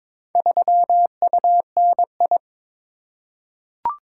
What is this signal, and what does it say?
Morse code, heard as a keyed tone, about 22 wpm, 700 Hz. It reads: 3UNI
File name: practice / 22wpm / Four Letter-Number Combinations - 22wpm